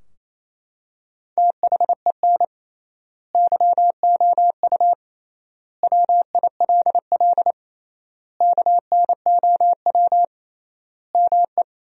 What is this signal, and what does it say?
Morse code, heard as a keyed tone, about 28 wpm, 700 Hz. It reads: THEN YOU WILL KNOW ME
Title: 3x Word-Spacing - Sentences from Top 400 Words - 28wpm